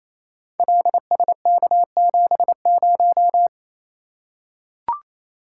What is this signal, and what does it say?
Morse code, heard as a keyed tone, about 28 wpm, 700 Hz. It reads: LSK70
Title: Code Groups - 5 Letter:Numbers - 28wpm